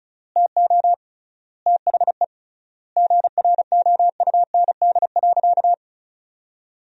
Morse code, heard as a keyed tone, about 34 wpm, 700 Hz. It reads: TO THE GROUND.
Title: 3x Word-Spacing - Sentences from Top 500 Words - 35wpm